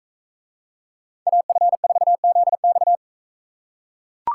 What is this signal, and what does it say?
Morse code, heard as a keyed tone, about 42 wpm, 700 Hz. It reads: AF4ZX